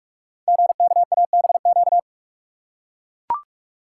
Morse code, heard as a keyed tone, about 44 wpm, 690 Hz. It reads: GKABQ
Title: Code Groups - 5 Letters - 45wpm